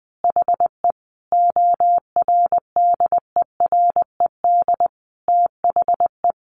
Morse code, heard as a keyed tone, about 20 wpm, 700 Hz. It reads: HE ORDERED THE
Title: Additional Easy Phrases 20wpm